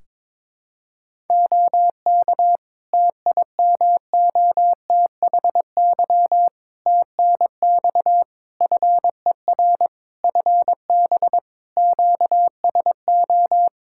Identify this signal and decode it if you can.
Morse code, heard as a keyed tone, about 22 wpm, 700 Hz. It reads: OK TIMOTHY TNX FER FB QSO
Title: K1USN SST QSO Practice 22wpm